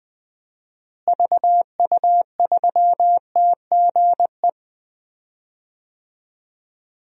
Morse code, heard as a keyed tone, about 20 wpm, 700 Hz. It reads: VU3TGE
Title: DXCC-100 Call Signs - 6 Characters - Repeated 2x - 20wpm